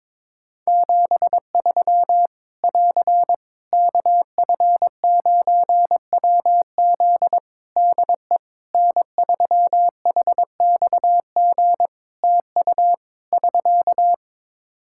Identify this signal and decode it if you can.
Morse code, heard as a keyed tone, about 21 wpm, 700 Hz. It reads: 73 <AR> KF9WZ DE N3HXG TU <SK>
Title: Practice QSO - Exchange 4 - Ending - 22wpm